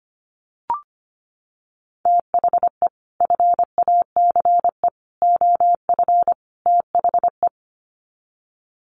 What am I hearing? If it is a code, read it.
Morse code, heard as a keyed tone, about 25 wpm, 700 Hz. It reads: THE FACE OF THE
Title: Sets of 4 Words 25wpm